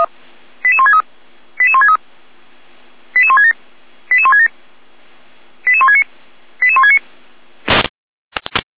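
Five-tone selective call (ZVEI), heard as quick runs of five tones, tones of about 70 ms. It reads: 88163 88163 88167 88167 88169 88169
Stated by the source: ZVEI five-tone selective call received on an SDR.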